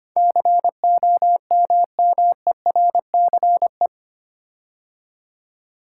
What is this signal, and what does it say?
Morse code, heard as a keyed tone, about 25 wpm, 700 Hz. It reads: COMMERCE
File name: 8 Letter Words - First 3 Letters Given - 25wpm